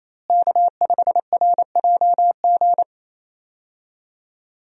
Morse code, heard as a keyed tone, about 28 wpm, 700 Hz. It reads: K5RJG
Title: USA Call Signs - 5 Characters - Repeated 3x - 28wpm